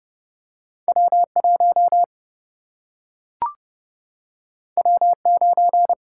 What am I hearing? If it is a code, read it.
Morse code, heard as a keyed tone, about 30 wpm, 700 Hz. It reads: W1 W9
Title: Single-Letter Single-Number 30wpm